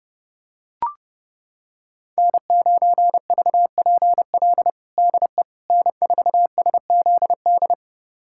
Morse code, heard as a keyed tone, about 32 wpm, 700 Hz. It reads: N9VPL DE N4SZD
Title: USA Callsign DE Callsign - 5 Characters - 30wpm